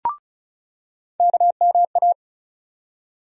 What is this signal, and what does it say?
Morse code, heard as a keyed tone, about 34 wpm, 700 Hz. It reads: KMA
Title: USA Call Sign Postfixes - 3 Letters 35wpm